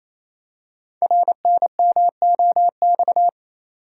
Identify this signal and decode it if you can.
Morse code, heard as a keyed tone, about 28 wpm, 700 Hz. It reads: RNMOX
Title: Code Groups - 5 Letters - 28wpm